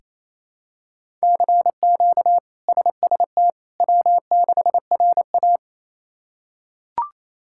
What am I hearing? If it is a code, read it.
Morse code, heard as a keyed tone, about 28 wpm, 700 Hz. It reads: CQ SST W6RA
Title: K1USN SST QSO Practice - 28wpm